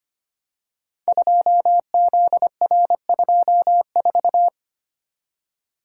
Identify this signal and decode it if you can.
Morse code, heard as a keyed tone, about 25 wpm, 700 Hz. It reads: 2ZR24
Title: Code Groups - 5 Letter:Numbers - 25wpm